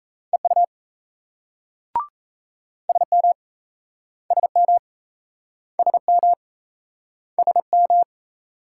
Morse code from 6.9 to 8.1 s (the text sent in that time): SM